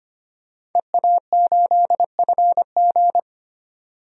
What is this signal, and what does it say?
Morse code, heard as a keyed tone, about 25 wpm, 700 Hz. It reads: EA8FG
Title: DXCC-100 Call Signs - 5 Characters - 25wpm